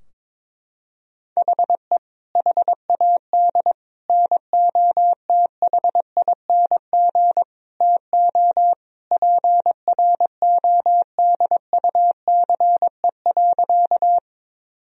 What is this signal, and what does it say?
Morse code, heard as a keyed tone, about 22 wpm, 700 Hz. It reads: HE HAD NOTHING TO PRODUCE.
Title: Sentences from Top 1000 Words - Review Part 3 - 22wpm